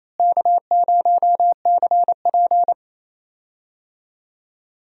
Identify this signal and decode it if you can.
Morse code, heard as a keyed tone, about 28 wpm, 700 Hz. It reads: K0CP
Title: USA Call Signs - 4 Characters - Repeated 2x - 28wpm